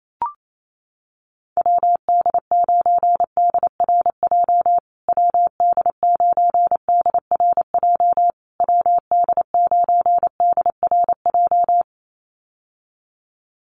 Morse code, heard as a keyed tone, about 28 wpm, 700 Hz. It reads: WD9DRJ WD9DRJ WD9DRJ
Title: USA Call Signs Repeated 3x 28wpm